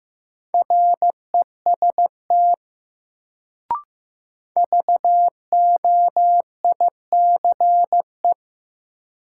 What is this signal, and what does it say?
Morse code, heard as a keyed tone, about 15 wpm, 700 Hz. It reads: REST VOICE
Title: Sentences from Top 500 Words - New Word Drill - 15wpm